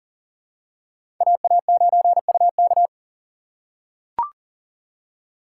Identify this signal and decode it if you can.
Morse code, heard as a keyed tone, about 40 wpm, 700 Hz. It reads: AA9UK
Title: USA Call Signs - 5 Characters - 3x Character Spacing - 40wpm